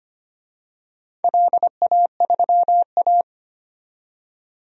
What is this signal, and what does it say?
Morse code, heard as a keyed tone, about 25 wpm, 700 Hz. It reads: LA3A
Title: DXCC-100 Call Signs - 4 Characters - 25wpm